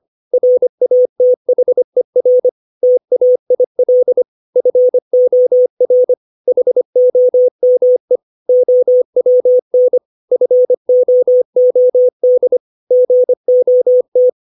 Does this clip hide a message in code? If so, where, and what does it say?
Yes: Morse code from 0.3 to 14.3 s, heard as a keyed tone, about 25 wpm, 500 Hz. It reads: RATHER TAIL FOR HOME OWN FOOD GOT